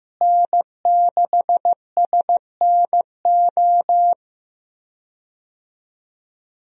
Morse code, heard as a keyed tone, about 16 wpm, 700 Hz. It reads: N6SNO
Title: USA Call Signs - 5 Characters - Repeated 2x - 15wpm